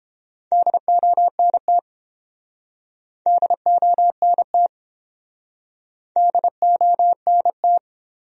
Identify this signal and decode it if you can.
Morse code, heard as a keyed tone, about 32 wpm, 700 Hz. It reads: DONT DONT DONT